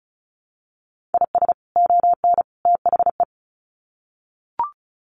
Morse code, heard as a keyed tone, about 34 wpm, 700 Hz. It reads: IS ON THE
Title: Sets of 3 Words 35wpm